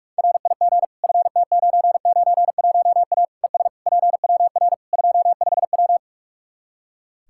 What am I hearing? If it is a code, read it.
Morse code, heard as a keyed tone, about 41 wpm, 700 Hz. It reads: RIG FT991A ES PWR 25W
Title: Practice QSO - Exchange 2 - Info - 45wpm